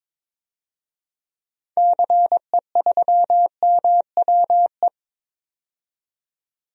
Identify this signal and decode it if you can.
Morse code, heard as a keyed tone, about 22 wpm, 700 Hz. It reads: CE3MWE
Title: DXCC-100 Call Signs - 6 Characters - 22wpm